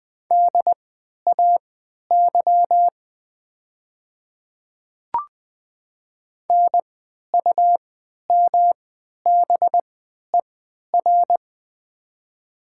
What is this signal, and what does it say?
Morse code, heard as a keyed tone, about 20 wpm, spaced as for 11 wpm, 700 Hz. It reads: DAY NUMBER
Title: 3x Character Spacing - Top 100 Words - 20wpm